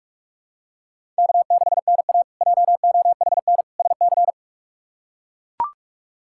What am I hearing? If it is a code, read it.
Morse code, heard as a keyed tone, about 41 wpm, 700 Hz. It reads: K6NA JOHN SC
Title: K1USN SST QSO Practice - 45wpm